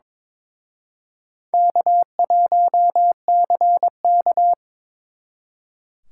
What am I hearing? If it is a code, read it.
Morse code, heard as a keyed tone, about 26 wpm, 700 Hz. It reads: K1CK
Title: Speed Racing - USA 1x2 Call Signs 22wpm